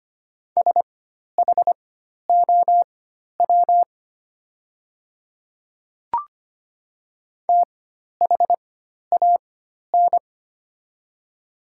Morse code, heard as a keyed tone, about 25 wpm, spaced as for 12 wpm, 700 Hz. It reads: SHOW THAN
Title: 4x Character Spacing - Top 300 Words - 25wpm